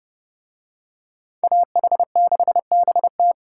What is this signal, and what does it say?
Morse code, heard as a keyed tone, about 27 wpm, 700 Hz. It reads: AH6BT